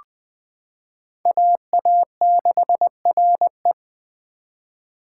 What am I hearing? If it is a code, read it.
Morse code, heard as a keyed tone, about 20 wpm, 700 Hz. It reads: AA6RE
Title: K1USN SST QSO Practice - 20wpm